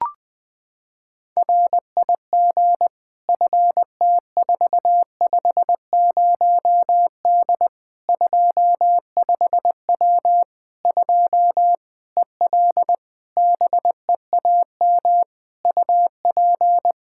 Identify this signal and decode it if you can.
Morse code, heard as a keyed tone, about 21 wpm, 700 Hz. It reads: RIG FT450D 25W 2 EL BEAM UP
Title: Practice QSO - Exchange 2 - Info - 20wpm